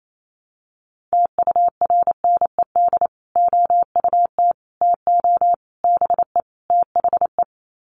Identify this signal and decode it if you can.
Morse code, heard as a keyed tone, about 28 wpm, 700 Hz. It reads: TURNED OUT TO BE THE
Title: Sets of 5 Words Encore 28wpm